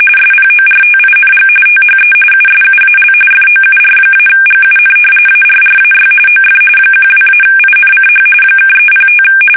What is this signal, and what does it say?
RTTY, about 45.45 baud. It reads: SPORTING THE USUAL FOUR-GUN ARRAY OF 12-INCH (305
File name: fsk850 / battleships